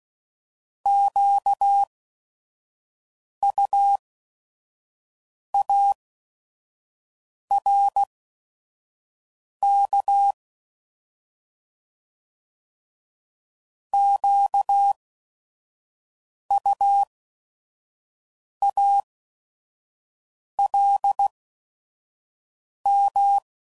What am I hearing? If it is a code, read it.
Morse code, heard as a keyed tone, about 16 wpm, spaced as for 5 wpm, 800 Hz. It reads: QUARK QUALM
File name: Prefixes / By First Letter / Qs